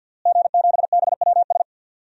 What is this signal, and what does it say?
Morse code, heard as a keyed tone, about 51 wpm, 700 Hz. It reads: G7BWS
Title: DXCC-100 Call Signs - 5 Characters - Repeated 3x - 50wpm